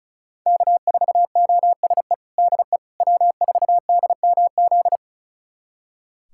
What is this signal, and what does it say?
Morse code, heard as a keyed tone, about 36 wpm, 700 Hz. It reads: K4OSE DE W4DMZ
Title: USA Callsign DE Callsign - 5 Characters - 35wpm